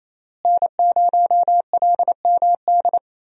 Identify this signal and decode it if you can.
Morse code, heard as a keyed tone, about 28 wpm, 700 Hz. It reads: N0LMD
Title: USA Call Signs - 5 Characters - Repeated 2x - 28wpm